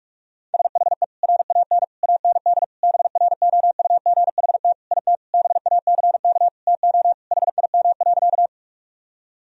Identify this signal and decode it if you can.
Morse code, heard as a keyed tone, about 42 wpm, 700 Hz. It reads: SHE RAN AND BROUGHT IT BACK TO HIM.
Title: Sentences from Top 1000 Words - Review Part 3 - 45wpm